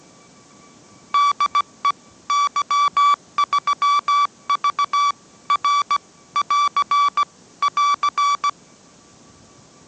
Morse code, heard as a keyed tone, about 17 wpm, 1150 Hz. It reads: DE Y3V R <AR> <AR>
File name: mrx / mrx080 18wpm